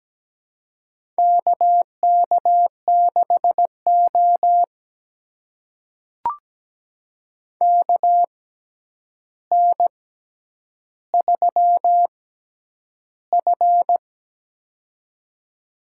Morse code, heard as a keyed tone, about 17 wpm, spaced as for 9 wpm, 700 Hz. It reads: KK6O KN3F
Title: USA Call Signs - 4 Characters - 6x Character Spacing - 17wpm